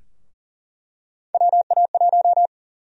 Morse code, heard as a keyed tone, about 40 wpm, 700 Hz. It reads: WA1
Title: USA Call Sign Prefixes - 40wpm